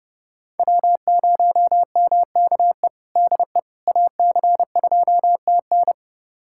Morse code, heard as a keyed tone, about 32 wpm, 700 Hz. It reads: W0MKE DE AC2TN
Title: USA Callsign DE Callsign - 5 Characters - 30wpm